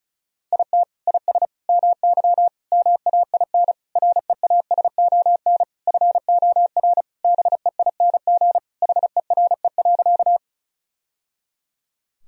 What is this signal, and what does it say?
Morse code, heard as a keyed tone, about 36 wpm, 700 Hz. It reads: IT IS MY MAIN REASON FOR BEING HERE.
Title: Sentences from Top 1000 Words - Review Part 2 - 35wpm